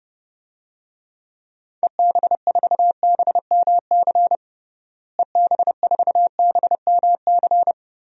Morse code, heard as a keyed tone, about 29 wpm, 700 Hz. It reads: EB4BMC EB4BMC
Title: DXCC-100 Call Signs - 6 Characters - Repeated 3x - 30wpm